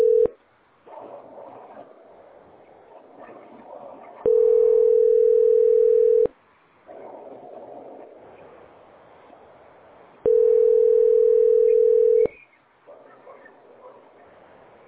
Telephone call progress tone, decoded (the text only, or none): ringback tone